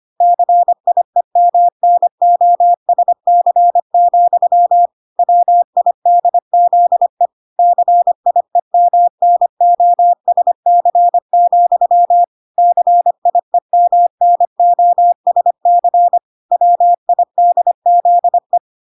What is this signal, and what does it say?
Morse code, heard as a keyed tone, about 25 wpm, 700 Hz. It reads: CIEMNOSC, WIDZE CIEMNOSC, CIEMNOSC WIDZE